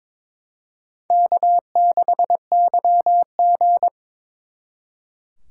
Morse code, heard as a keyed tone, about 25 wpm, 700 Hz. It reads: K6YG
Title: Speed Racing - USA 1x2 Call Signs 22wpm